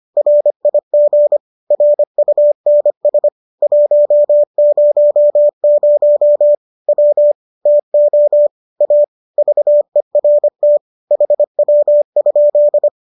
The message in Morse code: RIG RUNS 100 W TO A VERT HW?

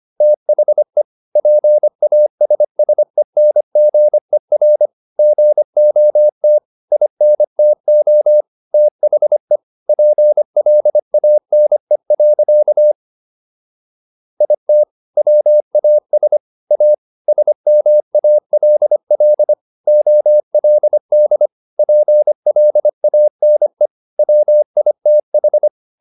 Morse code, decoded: THE PASSENGER GOT INTO THE PLANE. IT WAS A SMALL OLD PLANE WITH

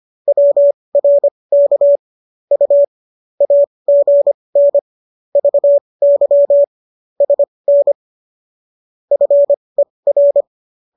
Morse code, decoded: WRK U AGN VY SN FER

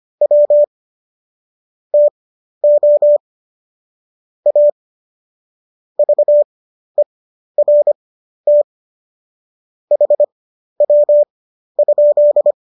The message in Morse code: W TO A VERT HW?

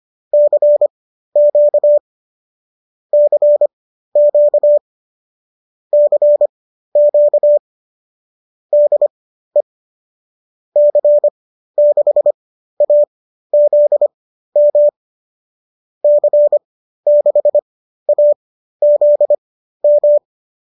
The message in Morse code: CQ CQ CQ DE C6AZM C6AZM

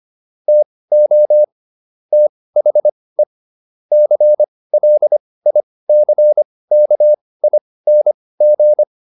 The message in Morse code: TO THE CLICKING

